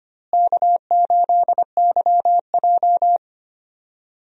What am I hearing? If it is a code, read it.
Morse code, heard as a keyed tone, about 25 wpm, 700 Hz. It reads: K8YJ
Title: USA Call Signs - 4 Characters - Repeated 3x - 25wpm